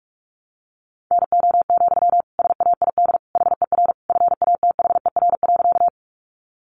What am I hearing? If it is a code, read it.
Morse code, heard as a keyed tone, about 44 wpm, 700 Hz. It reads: NO, SAID HER FATHER.
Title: Sentences from Top 300 Words 45wpm